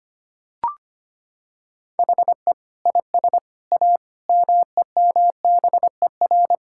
Morse code, heard as a keyed tone, about 25 wpm, 700 Hz. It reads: HE IS A MEMBER
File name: practice / 25wpm / Sentences with 3-Word Phrases - 25wpm